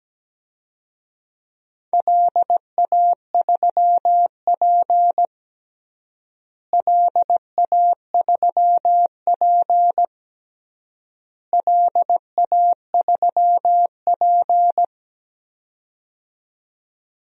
Morse code, heard as a keyed tone, about 17 wpm, 700 Hz. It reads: LA3P LA3P LA3P
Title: DXCC-100 Call Signs - 4 Characters - Repeated 3x - 17wpm